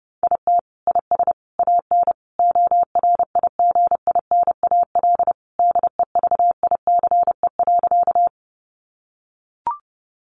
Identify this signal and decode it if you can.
Morse code, heard as a keyed tone, about 28 wpm, 700 Hz. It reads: IT IS AN ORIGINAL DEVICE.
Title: Sentences from Top 1500 Words 30wpm